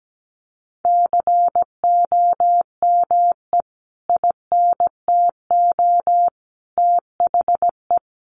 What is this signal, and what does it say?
Morse code, heard as a keyed tone, about 17 wpm, 700 Hz. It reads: COME INTO THE
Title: Sentences from Top 1400 Words 17wpm